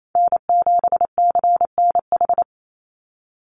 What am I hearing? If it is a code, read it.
Morse code, heard as a keyed tone, about 28 wpm, 700 Hz. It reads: N7CNH
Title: NA SOTA Chaser Callsigns 28wpm